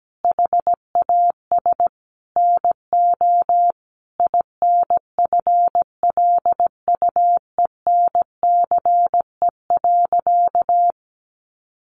Morse code, heard as a keyed tone, about 17 wpm, 700 Hz. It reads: HAS NO INFLUENCE.
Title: Sentences from Top 1100 Words 17wpm